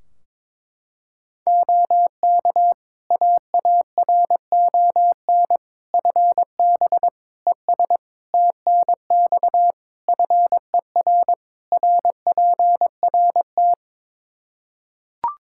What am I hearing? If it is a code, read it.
Morse code, heard as a keyed tone, about 21 wpm, 700 Hz. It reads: OK AARON FB ES TNX FER RPRT
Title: K1USN SST QSO Practice 22wpm